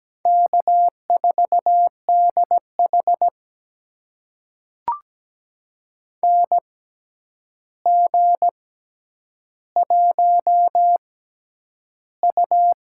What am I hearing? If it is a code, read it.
Morse code, heard as a keyed tone, about 17 wpm, spaced as for 9 wpm, 700 Hz. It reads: K4DH NG1U
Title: USA Call Signs - 4 Characters - 6x Character Spacing - 17wpm